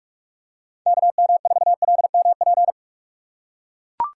Morse code, heard as a keyed tone, about 41 wpm, 700 Hz. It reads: KM4LMP
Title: USA Call Signs - 6 Characters - 3x Character Spacing - 45wpm